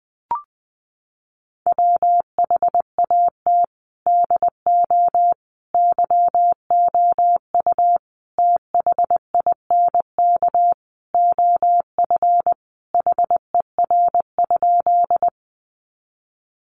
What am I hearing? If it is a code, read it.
Morse code, heard as a keyed tone, about 20 wpm, 700 Hz. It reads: WHAT DO YOU THINK OF HER?
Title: Sentences from Top 200 Words 20wpm